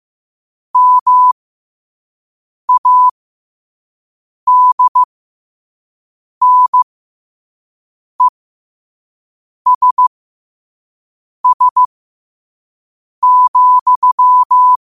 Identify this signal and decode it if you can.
Morse code, heard as a keyed tone, about 15 wpm, spaced as for 5 wpm, 1000 Hz. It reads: MADNESS,
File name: qotdfiles / 5wpm